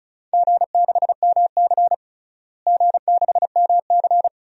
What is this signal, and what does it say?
Morse code, heard as a keyed tone, about 34 wpm, 700 Hz. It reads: G6MC G6MC
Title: DXCC-100 Call Signs - 4 Characters - Repeated 2x - 35wpm